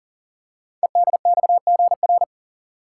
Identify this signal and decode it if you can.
Morse code, heard as a keyed tone, about 42 wpm, 700 Hz. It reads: EDXGR
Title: Code Groups - 5 Letters - 40wpm